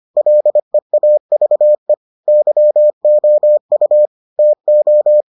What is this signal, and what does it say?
Morse code, heard as a keyed tone, about 25 wpm, 600 Hz. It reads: LEAVE YOU TO